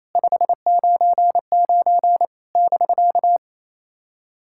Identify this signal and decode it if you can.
Morse code, heard as a keyed tone, about 28 wpm, 700 Hz. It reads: 599 <BK>
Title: SOTA QSO Practice 28wpm